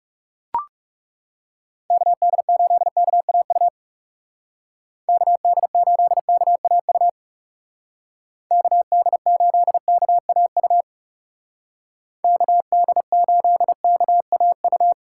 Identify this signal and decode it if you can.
Morse code, heard as a keyed tone, about 35 wpm, 700 Hz. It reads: KD8KAU KD8KAU KD8KAU KD8KAU
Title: Speed Racing - USA 2x3 Call Signs 30wpm